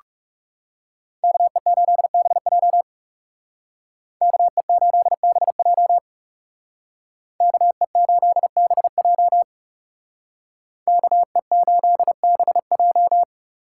Morse code, heard as a keyed tone, about 39 wpm, 700 Hz. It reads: KE8BJ KE8BJ KE8BJ KE8BJ